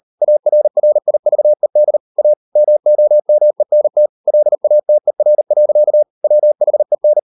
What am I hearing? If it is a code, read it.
Morse code, heard as a keyed tone, about 40 wpm, 600 Hz. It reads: ARRIVED A MOMENT LATER. WHEN